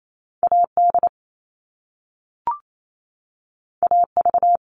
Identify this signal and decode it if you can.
Morse code, heard as a keyed tone, about 28 wpm, 700 Hz. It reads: AD AV
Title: All Cut Numbers - 2-Digit 28wpm